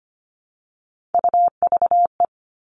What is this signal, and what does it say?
Morse code, heard as a keyed tone, about 25 wpm, 700 Hz. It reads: UVE
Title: All Cut Numbers - 3-Digit 25wpm